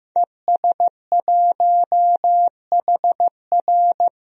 Morse code, heard as a keyed tone, about 16 wpm, 700 Hz. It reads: ES1HR
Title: DXCC-100 Call Signs - 5 Characters - Repeated 3x - 15wpm